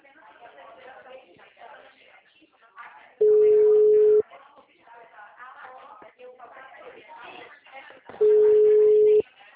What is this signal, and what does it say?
Ringback tone (Europe, 425 Hz), with speech.
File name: out- -2009- - - .49518